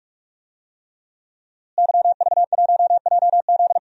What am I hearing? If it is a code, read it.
Morse code, heard as a keyed tone, about 46 wpm, 700 Hz. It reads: YV1JZ
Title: DXCC-100 Call Signs - 5 Characters - Repeated 3x - 45wpm